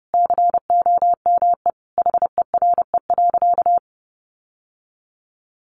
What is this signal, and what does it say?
Morse code, heard as a keyed tone, about 28 wpm, 700 Hz. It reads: COME HERE.
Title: Sentences from Top 1300 Words 30wpm